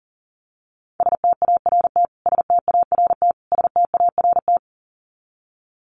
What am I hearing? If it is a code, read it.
Morse code, heard as a keyed tone, about 40 wpm, 700 Hz. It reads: START START START
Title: Top 300 Words Repeated 40wpm